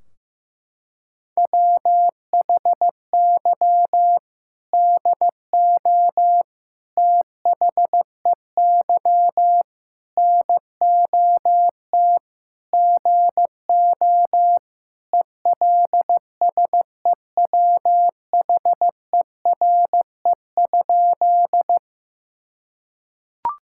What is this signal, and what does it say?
Morse code, heard as a keyed tone, about 15 wpm, 700 Hz. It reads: WHY DO THEY NOT GO ELSEWHERE?
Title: Sentences from Top 1600 Words 15wpm